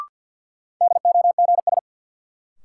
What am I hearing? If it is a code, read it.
Morse code, heard as a keyed tone, about 48 wpm, 700 Hz. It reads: DOGS
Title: Sentences from Top 1600 Words - New Word Drill - 50wpm